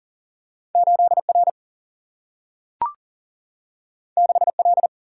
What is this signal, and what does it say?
Morse code, heard as a keyed tone, about 42 wpm, 700 Hz. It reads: 8 R 6 L
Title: Single-Letter Single-Number 40wpm